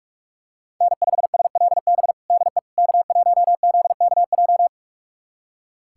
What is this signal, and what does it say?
Morse code, heard as a keyed tone, about 46 wpm, 700 Hz. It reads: N5SLB DE K1ZKJ
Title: USA Callsign DE Callsign 45wpm